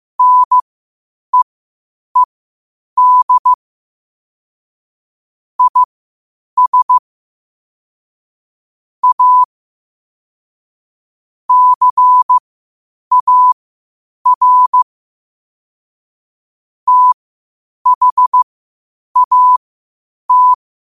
Morse code, heard as a keyed tone, about 15 wpm, spaced as for 8 wpm, 1000 Hz. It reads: NEED IS A CAR THAT
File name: qotdfiles / 8wpm